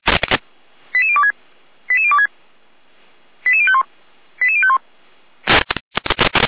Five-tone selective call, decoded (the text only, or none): 88026 88026 88051 88051